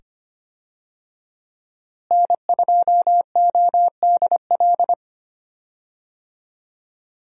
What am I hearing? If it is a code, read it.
Morse code, heard as a keyed tone, about 25 wpm, 700 Hz. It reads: N2ODL
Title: USA Call Signs - 5 Characters - Repeated 2x - 25wpm